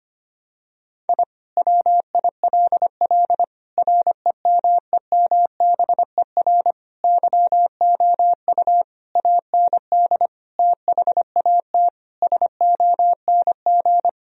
Morse code, heard as a keyed tone, about 25 wpm, 700 Hz. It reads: I WILL REMEMBER YOU AND THAT SONG
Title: Sentences from Top 1000 Words - Review Part 2 - 25wpm